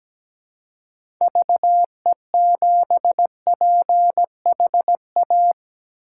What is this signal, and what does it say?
Morse code, heard as a keyed tone, about 17 wpm, 700 Hz. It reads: VE7PHA
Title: K1USN SST QSO Practice - 17wpm